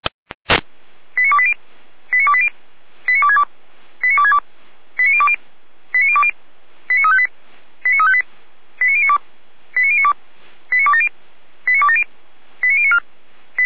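Five-tone selective call (ZVEI), heard as quick runs of five tones, tones of about 70 ms. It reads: 89280 89280 89262 89262 89020 89020 89358 89358 89092 89092 89270 89270 89094
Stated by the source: ZVEI five-tone selective call received by SDR.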